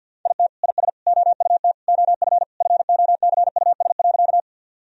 Morse code, heard as a keyed tone, about 50 wpm, 700 Hz. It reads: IT IS OUT OF FOCUS.
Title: Sentences from Top 1000 Words - Review Part 4 - 50wpm